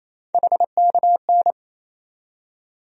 Morse code, heard as a keyed tone, about 28 wpm, 700 Hz. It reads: HKN
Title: USA Call Sign Postfixes - 3 Letters 28wpm